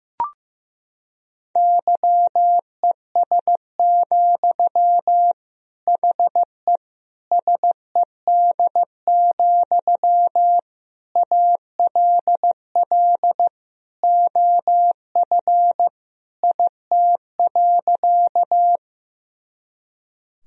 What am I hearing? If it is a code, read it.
Morse code, heard as a keyed tone, about 15 wpm, 700 Hz. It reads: YES, HE SED, ALL OF IT.
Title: Sentences from Top 500 Words and Abbreviations 15wpm